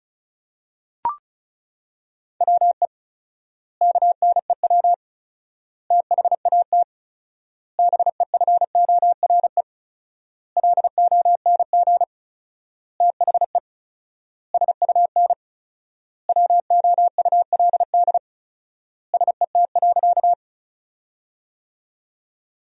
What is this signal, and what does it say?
Morse code, heard as a keyed tone, about 34 wpm, 700 Hz. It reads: WE KNEW THAT BEFORE LONG THE SUN WOULD SET.
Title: 4x Word-Spacing - Sentences from Top 400 Words - 35wpm